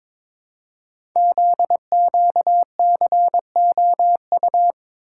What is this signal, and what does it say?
Morse code, heard as a keyed tone, about 22 wpm, 700 Hz. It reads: ZQCOU
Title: Code Groups - 5 Letters - 22wpm